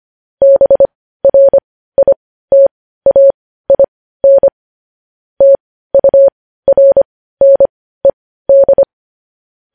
Morse code, heard as a keyed tone, about 25 wpm, spaced as for 15 wpm, 550 Hz. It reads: BRITAIN TURNED
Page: News Headlines in Morse Code at 15 WPM